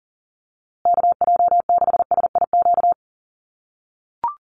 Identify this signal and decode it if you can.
Morse code, heard as a keyed tone, about 40 wpm, 700 Hz. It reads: KJ6SIQ
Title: USA Call Signs Repeated 2x 40wpm